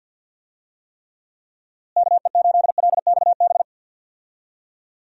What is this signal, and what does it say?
Morse code, heard as a keyed tone, about 48 wpm, 700 Hz. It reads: KE8LXB
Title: USA Call Signs - 6 Characters - Repeated 2x - 50wpm